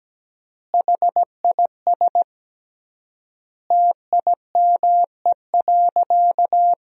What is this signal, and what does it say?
Morse code, heard as a keyed tone, about 17 wpm, 700 Hz. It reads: HIS TIME.
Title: 3x Word-Spacing - Sentences from Top 400 Words - 17wpm